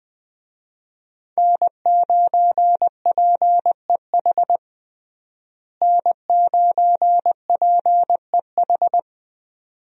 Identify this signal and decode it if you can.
Morse code, heard as a keyed tone, about 21 wpm, 700 Hz. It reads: N9PEH N9PEH
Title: USA Call Signs - 5 Characters - Repeated 2x - 20wpm